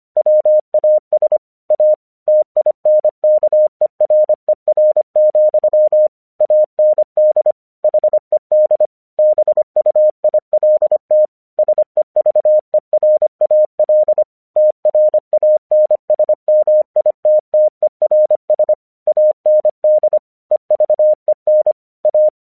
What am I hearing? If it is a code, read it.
Morse code, heard as a keyed tone, about 24 wpm, 610 Hz. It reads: WAS A TINKERER, AND HED BUILT SEVERAL TRANSMITTERS AND EVEN A
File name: 1 English 25wpm / AQC11